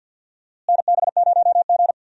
Morse code, heard as a keyed tone, about 53 wpm, 700 Hz. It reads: NB0G
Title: USA Call Signs - 4 Characters - Repeated 3x - 50wpm